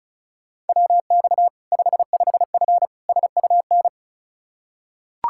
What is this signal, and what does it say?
Morse code, heard as a keyed tone, about 36 wpm, 700 Hz. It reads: WX 55F SUN
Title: K1USN SST QSO Practice 35wpm